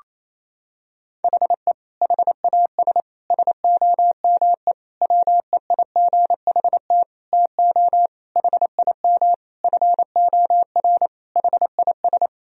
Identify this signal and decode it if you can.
Morse code, heard as a keyed tone, about 28 wpm, 700 Hz. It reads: HE HAS SOME WEIGHT TO HIM FOR HIS
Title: Sentences from Top 1000 Words - Review Part 3 - 28wpm